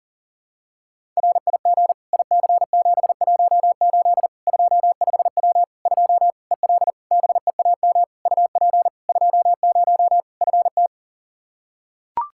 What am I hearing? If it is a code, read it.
Morse code, heard as a keyed tone, about 40 wpm, 700 Hz. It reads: RIG IC718 25W 2 EL BEAM UP 20 FT
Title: Practice QSO - Exchange 2 - Info - 40wpm